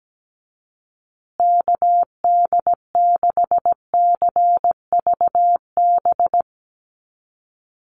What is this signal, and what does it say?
Morse code, heard as a keyed tone, about 17 wpm, 700 Hz. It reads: KD6CVB
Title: USA Call Signs Repeated 3x 17wpm